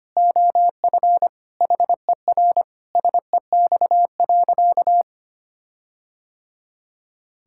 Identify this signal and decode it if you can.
Morse code, heard as a keyed tone, about 25 wpm, 700 Hz. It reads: OF HER SEX.
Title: Sentences from Top 1600 Words 25wpm